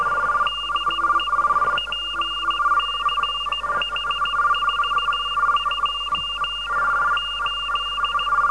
Morse code, heard as a keyed tone, about 16 wpm, 2800 Hz, with noise. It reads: DE PCH42 8